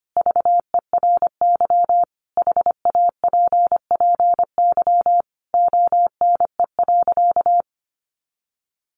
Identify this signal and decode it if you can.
Morse code, heard as a keyed tone, about 25 wpm, 700 Hz. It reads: VERY HAPPY ONE.
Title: Sentences from Top 1400 Words 25wpm